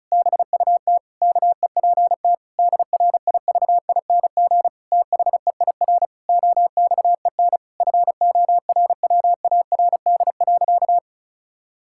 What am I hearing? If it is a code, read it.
Morse code, heard as a keyed tone, about 34 wpm, 700 Hz. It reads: BUT KEPT DRIVING THEIR OXEN FORWARD.
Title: Pangram Challenge v2 - 35wpm